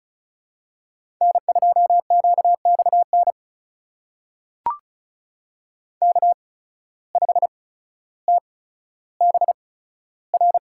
Morse code, heard as a keyed tone, about 36 wpm, spaced as for 15 wpm, 700 Hz. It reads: N2QXN K5TBR
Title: USA Call Signs - 5 Characters - 8x Character Spacing - 35wpm